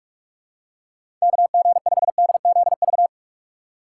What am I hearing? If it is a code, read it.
Morse code, heard as a keyed tone, about 44 wpm, 700 Hz. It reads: KG5DZV
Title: USA Call Signs - 6 Characters - 6x Character Spacing - 45wpm